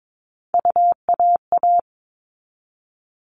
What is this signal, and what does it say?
Morse code, heard as a keyed tone, about 22 wpm, 700 Hz. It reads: UAA
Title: All Cut Numbers - 3-Digit 22wpm